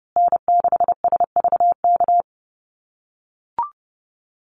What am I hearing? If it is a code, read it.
Morse code, heard as a keyed tone, about 30 wpm, 700 Hz. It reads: N6SVK
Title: USA Call Signs Repeated 2x 30wpm